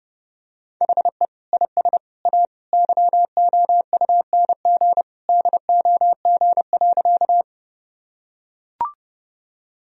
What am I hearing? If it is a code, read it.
Morse code, heard as a keyed tone, about 30 wpm, 700 Hz. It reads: HE IS A YOUNG DOG.
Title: Sentences from Top 1000 Words - Review Part 2 - 30wpm